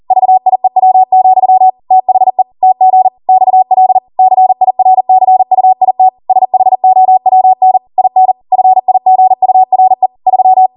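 Morse code, heard as a keyed tone, about 40 wpm, 750 Hz. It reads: VIEW, THE TG XL CIRCUIT SHOWN IN FIGURE 3
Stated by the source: ARRL 40 WPM Morse code practice transmission.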